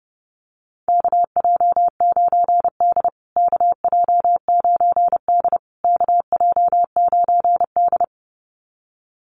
Morse code, heard as a keyed tone, about 30 wpm, 700 Hz. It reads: KJ9D KJ9D KJ9D
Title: USA Call Signs Repeated 3x 30wpm